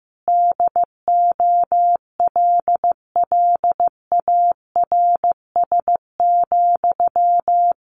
Morse code, heard as a keyed tone, about 15 wpm, 700 Hz. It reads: DOLLARS,
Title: A Sound Of Thunder 15wpm